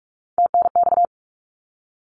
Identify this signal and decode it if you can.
Morse code, heard as a keyed tone, about 46 wpm, 700 Hz. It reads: TNX
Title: Abbreviations Repeated 3x - 45wpm